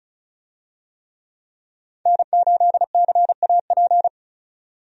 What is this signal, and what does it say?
Morse code, heard as a keyed tone, about 36 wpm, 700 Hz. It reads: N8CAP
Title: USA Call Signs - 5 Characters - 35wpm